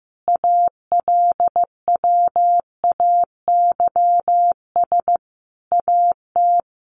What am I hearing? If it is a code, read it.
Morse code, heard as a keyed tone, about 15 wpm, 700 Hz. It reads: ALWAYS AT
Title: Sentences from Top 1200 Words 15wpm